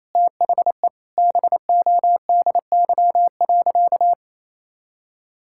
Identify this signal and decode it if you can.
Morse code, heard as a keyed tone, about 28 wpm, 700 Hz. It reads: THE BODY.
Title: Sentences from Top 1000 Words - Review Part 6 - 28wpm